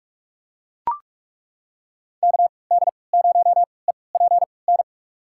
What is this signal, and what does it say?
Morse code, heard as a keyed tone, about 43 wpm, spaced as for 25 wpm, 700 Hz. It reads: KD0EPN
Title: USA Call Signs - 6 Characters - 3x Character Spacing - 45wpm